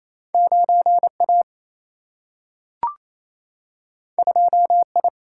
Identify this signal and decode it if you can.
Morse code, heard as a keyed tone, about 28 wpm, 700 Hz. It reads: 9 A 2 I
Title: Single-Letter Single-Number 28wpm